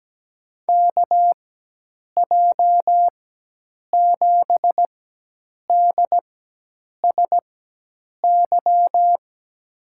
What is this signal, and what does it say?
Morse code, heard as a keyed tone, about 17 wpm, spaced as for 8 wpm, 700 Hz. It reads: KJ7DSY
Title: USA Call Signs - 6 Characters - 4x Character Spacing - 17wpm